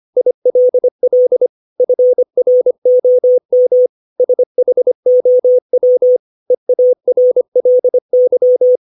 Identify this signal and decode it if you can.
Morse code, heard as a keyed tone, about 25 wpm, 500 Hz. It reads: ILL FROM SHOW EARLY
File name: 632 words / 632 Words 25WPM